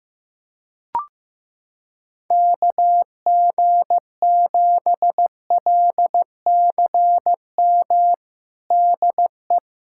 Morse code, heard as a keyed tone, about 15 wpm, 700 Hz. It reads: KG7LCM DE
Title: USA Callsign DE Callsign - 6 Characters - 15wpm